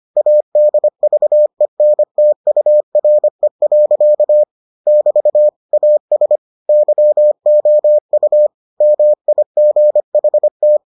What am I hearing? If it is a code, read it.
Morse code, heard as a keyed tone, about 25 wpm, 600 Hz. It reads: ADVENTURE. <BT> AS YOU MIGHT